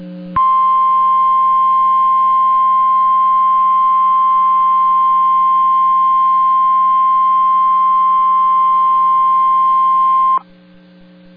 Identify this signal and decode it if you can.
Emergency Alert System (SAME) tones (the 1050 Hz attention tone). It is heard from 0.3 to 10.4 s.